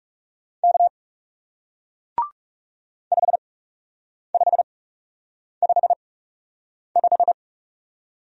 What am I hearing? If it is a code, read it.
Morse code, heard as a keyed tone, about 37 wpm, spaced as for 8 wpm, 700 Hz. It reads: K 5555
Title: Speed-Racing Single Letter-Number 30wpm